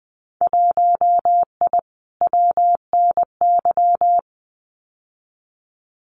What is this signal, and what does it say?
Morse code, heard as a keyed tone, about 20 wpm, 700 Hz. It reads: 1I WNY
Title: Winter Field Day Exchanges Encore 20wpm